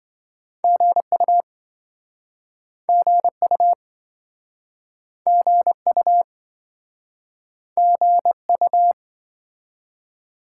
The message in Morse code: GU GU GU GU

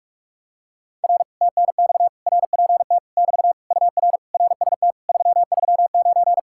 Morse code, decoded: R TNX RPT <BT> UR RST 339

